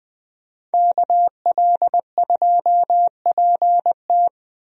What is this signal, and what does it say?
Morse code, heard as a keyed tone, about 20 wpm, 700 Hz. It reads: KL2PT
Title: USA Call Signs - 5 Characters - 20wpm